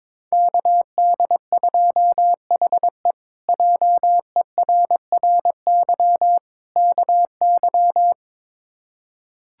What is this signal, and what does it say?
Morse code, heard as a keyed tone, about 22 wpm, 700 Hz. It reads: KD2HE JERRY KY
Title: K1USN SST QSO Practice - 22wpm